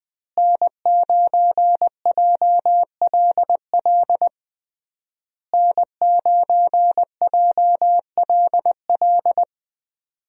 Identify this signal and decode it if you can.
Morse code, heard as a keyed tone, about 20 wpm, 700 Hz. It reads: N9JLL N9JLL
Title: USA Call Signs - 5 Characters - Repeated 3x - 20wpm